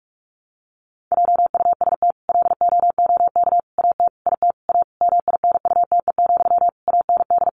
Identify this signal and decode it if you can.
Morse code, heard as a keyed tone, about 44 wpm, 700 Hz. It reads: JUST LOOK AT IT A MINUTE, AND
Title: Sentences from Top 1000 Words 45wpm